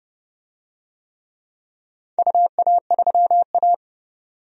Morse code, heard as a keyed tone, about 31 wpm, 700 Hz. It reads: UA3A
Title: DXCC-100 Call Signs - 4 Characters - 30wpm